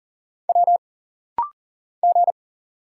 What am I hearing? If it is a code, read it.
Morse code, heard as a keyed tone, about 40 wpm, 700 Hz. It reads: W G
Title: Single Letters - Rapid-Fire 40wpm